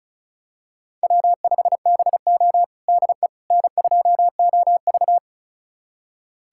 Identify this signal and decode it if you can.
Morse code, heard as a keyed tone, about 34 wpm, 700 Hz. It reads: W5BO DE N2OV
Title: USA Callsign DE Callsign - 4 Characters - 35wpm